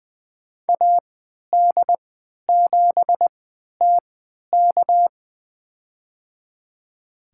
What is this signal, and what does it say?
Morse code, heard as a keyed tone, about 20 wpm, spaced as for 11 wpm, 700 Hz. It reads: AD7TK
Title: USA Call Signs - 5 Characters - 3x Character Spacing - 20wpm